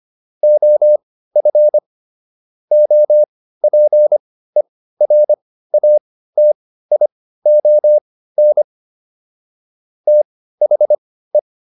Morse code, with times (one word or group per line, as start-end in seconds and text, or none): 0.4-1.8 OF
2.7-8.6 OPERATION
10.1-11.4 THE